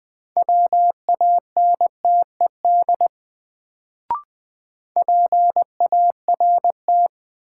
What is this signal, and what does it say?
Morse code, heard as a keyed tone, about 20 wpm, 700 Hz. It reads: WANTED PART
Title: Sets of 2 Words - New Word Drill - 20wpm